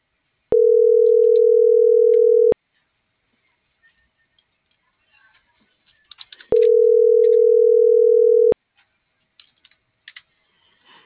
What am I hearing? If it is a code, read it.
Ringback tone (North America), 440 + 480 Hz.